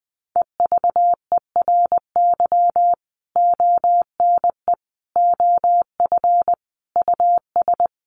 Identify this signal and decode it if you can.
Morse code, heard as a keyed tone, about 20 wpm, 700 Hz. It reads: EVERY ONE OF US
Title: Sentences from Top 1500 Words 20wpm